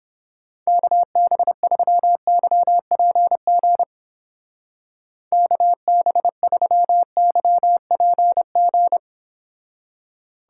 Morse code, heard as a keyed tone, about 31 wpm, 700 Hz. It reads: KB3YPG KB3YPG